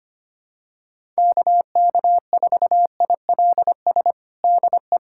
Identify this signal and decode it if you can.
Morse code, heard as a keyed tone, about 25 wpm, 700 Hz. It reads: KK4ILS DE
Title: USA Callsign DE Callsign - 6 Characters - 25wpm